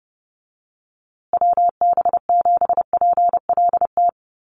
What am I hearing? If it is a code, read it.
Morse code, heard as a keyed tone, about 30 wpm, 700 Hz. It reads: WB7PLT
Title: USA Call Signs Repeated 3x 30wpm